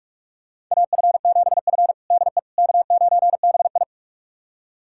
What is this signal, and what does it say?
Morse code, heard as a keyed tone, about 42 wpm, 700 Hz. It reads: AF7F DE K9BI
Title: USA Callsign DE Callsign - 4 Characters - 45wpm